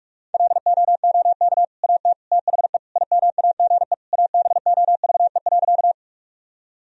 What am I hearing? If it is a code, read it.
Morse code, heard as a keyed tone, about 43 wpm, 700 Hz. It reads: LOOK AT THE IMAGE ABOVE.
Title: Sentences from Top 1000 Words - Review Part 4 - 45wpm